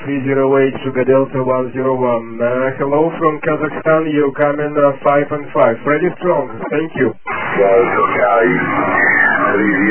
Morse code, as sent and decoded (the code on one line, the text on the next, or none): none